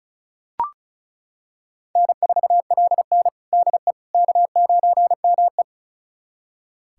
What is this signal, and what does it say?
Morse code, heard as a keyed tone, about 36 wpm, 700 Hz. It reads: N4LN DE K9ME
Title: USA Callsign DE Callsign - 4 Characters - 35wpm